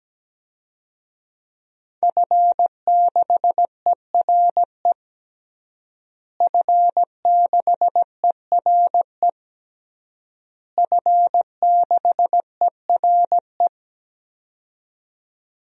Morse code, heard as a keyed tone, about 17 wpm, 700 Hz. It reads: F6ERE F6ERE F6ERE
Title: DXCC-100 Call Signs - 5 Characters - Repeated 3x - 17wpm